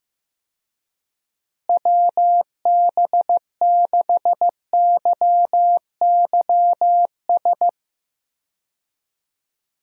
Morse code, heard as a keyed tone, about 16 wpm, 700 Hz. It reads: WB6YYS
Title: USA Call Signs - 6 Characters - Repeated 2x - 15wpm